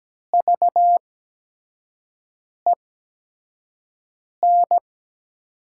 Morse code, heard as a keyed tone, about 17 wpm, spaced as for 5 wpm, 700 Hz. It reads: VEN
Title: 8x Character Spacing - Top 200 Words - 17wpm